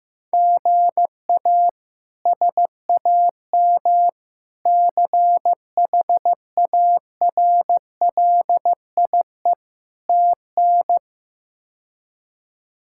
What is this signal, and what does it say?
Morse code, heard as a keyed tone, about 15 wpm, 700 Hz. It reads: GA SAM CHARLIE TN
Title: K1USN SST QSO Practice - 15wpm